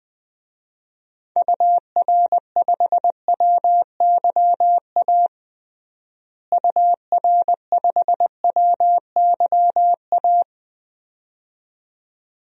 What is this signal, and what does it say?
Morse code, heard as a keyed tone, about 20 wpm, 700 Hz. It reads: UR5WYA UR5WYA
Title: DXCC-100 Call Signs - 6 Characters - Repeated 2x - 20wpm